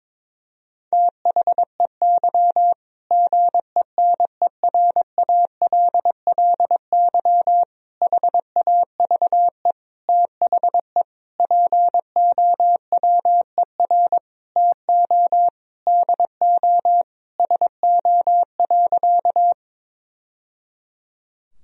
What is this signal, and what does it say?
Morse code, heard as a keyed tone, about 22 wpm, 700 Hz. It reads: THEY GENERALLY HAVE THE POWER TO DO SO.
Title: Sentences from Top 1000 Words - Review Part 4 - 22wpm